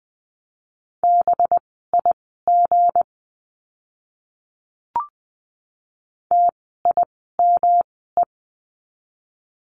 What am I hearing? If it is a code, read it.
Morse code, heard as a keyed tone, about 20 wpm, spaced as for 14 wpm, 700 Hz. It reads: BIG TIME
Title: 237 Lesson 38f - Letter q - Words 20wpm